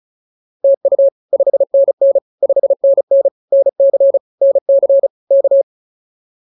Morse code, heard as a keyed tone, about 34 wpm, 560 Hz. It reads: TU 5NN 5NN NC NC K